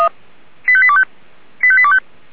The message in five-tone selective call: 85825 85825

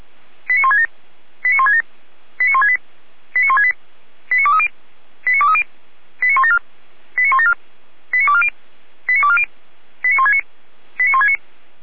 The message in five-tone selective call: 89167 89167 89168 89168 89230 89230 89174 89174 89240 89240 89169 89169

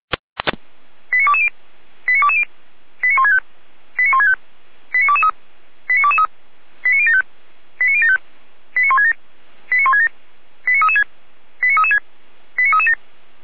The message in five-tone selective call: 89220 89220 89165 89165 89223 89223 89075 89075 89167 89167 89336 89336 89337

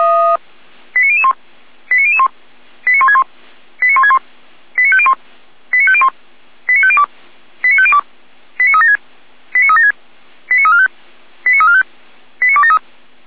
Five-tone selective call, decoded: 89001 89001 89161 89161 89501 89501 89502 89502 89376 89376 89345 89345 89273

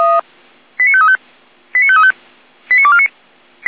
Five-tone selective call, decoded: 89535 89535 89249

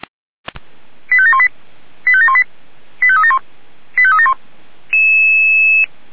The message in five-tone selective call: 85718 85718 85371 85371

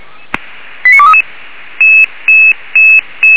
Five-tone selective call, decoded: 89233

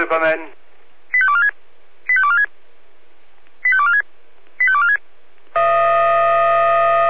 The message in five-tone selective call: 85246 85246 85246 85246

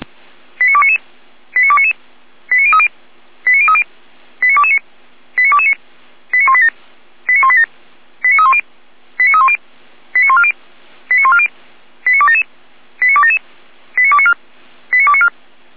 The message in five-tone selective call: 89299 89299 89030 89030 89229 89229 89187 89187 89310 89310 89140 89140 89277 89277 89294 89294

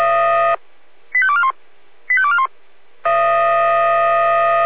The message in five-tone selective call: 85241 85241

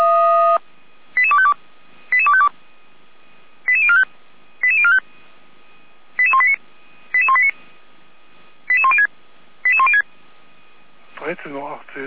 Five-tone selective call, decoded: 88252 88252 88045 88045 88189 88189 88106 88106

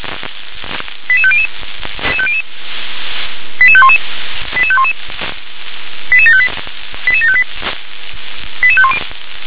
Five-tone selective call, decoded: 88500 88511 88511 88758 88758 88510

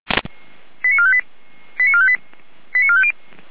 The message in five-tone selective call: 89458 89458 89450

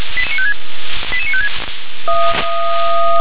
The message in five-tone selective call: 88956

